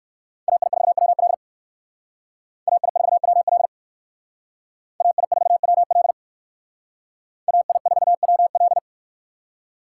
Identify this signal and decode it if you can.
Morse code, heard as a keyed tone, about 58 wpm, 700 Hz. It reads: AI4WL AI4WL AI4WL AI4WL